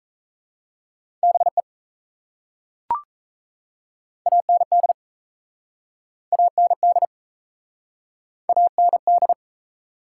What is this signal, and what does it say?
Morse code, heard as a keyed tone, about 37 wpm, 700 Hz. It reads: BE AND AND AND